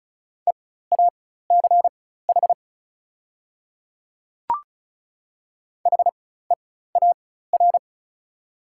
Morse code, heard as a keyed tone, about 34 wpm, spaced as for 16 wpm, 690 Hz. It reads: EACH HEAR
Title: 4x Character Spacing - Top 300 Words - 35wpm